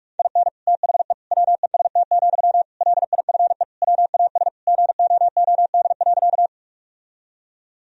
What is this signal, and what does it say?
Morse code, heard as a keyed tone, about 46 wpm, 700 Hz. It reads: IN THE WEST, LIFE WAS GOOD.
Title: Sentences from Top 1000 Words - Review Part 2 - 45wpm